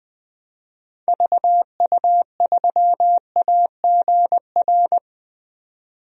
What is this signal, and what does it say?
Morse code, heard as a keyed tone, about 21 wpm, 700 Hz. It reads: VU3AGR
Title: DXCC-100 Call Signs - 6 Characters - 20wpm